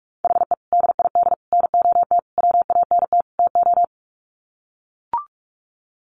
Morse code, heard as a keyed tone, about 44 wpm, 700 Hz. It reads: HE DID NOT WANT TO
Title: Sets of 5 Words Encore 45wpm